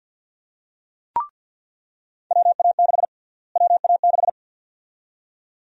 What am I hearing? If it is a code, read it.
Morse code, heard as a keyed tone, about 48 wpm, 700 Hz. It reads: WA6 WA6
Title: USA Call Sign Prefixes - Repeated 2x - 50wpm